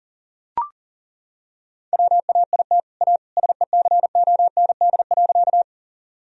Morse code, heard as a keyed tone, about 40 wpm, 700 Hz. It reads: WAIT A SECOND.
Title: Sentences from Top 1000 Words - Review Part 1 - 40wpm